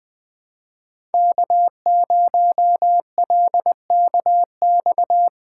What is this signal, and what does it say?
Morse code, heard as a keyed tone, about 20 wpm, 700 Hz. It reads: K0LKX
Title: USA Call Signs - 5 Characters - 2x Character Spacing - 20wpm